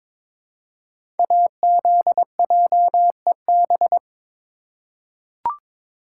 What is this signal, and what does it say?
Morse code, heard as a keyed tone, about 22 wpm, 700 Hz. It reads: AZJEB
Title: Code Groups - 5 Letter:Numbers - 22wpm